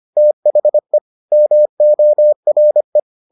Morse code, heard as a keyed tone, about 25 wpm, 600 Hz. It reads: THE MORE